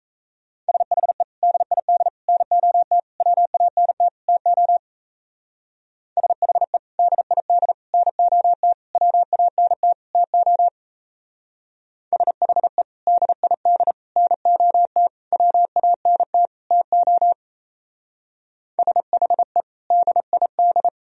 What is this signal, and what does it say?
Morse code, heard as a keyed tone, about 33 wpm, 700 Hz. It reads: SHE DID NOT WANT TO SHE DID NOT WANT TO SHE DID NOT WANT TO SHE DID
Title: Speed-Racing - Sets of 5 Words 28wpm